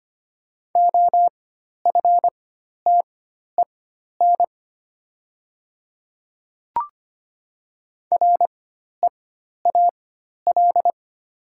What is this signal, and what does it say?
Morse code, heard as a keyed tone, about 25 wpm, spaced as for 12 wpm, 700 Hz. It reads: OFTEN REAL
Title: 4x Character Spacing - Top 300 Words - 25wpm